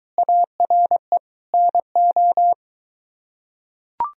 Morse code, heard as a keyed tone, about 22 wpm, 700 Hz. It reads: ARE NO